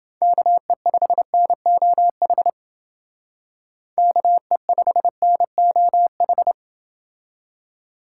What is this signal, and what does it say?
Morse code, heard as a keyed tone, about 28 wpm, 700 Hz. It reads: KE5NOH KE5NOH